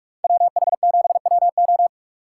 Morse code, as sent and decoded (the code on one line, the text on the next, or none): .-- .... --... .-- ---
WH7WO